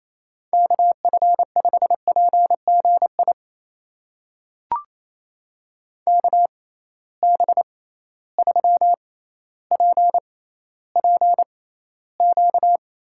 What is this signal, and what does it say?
Morse code, heard as a keyed tone, about 28 wpm, spaced as for 14 wpm, 700 Hz. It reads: KF5PGI KB3PPQ
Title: USA Call Signs - 6 Characters - 6x Character Spacing - 28wpm